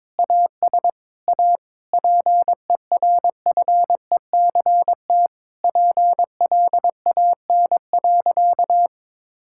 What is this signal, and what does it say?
Morse code, heard as a keyed tone, about 21 wpm, 700 Hz. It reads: AS A PERFECT PLAN.
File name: Sentences with 7-Word Phrases - 22wpm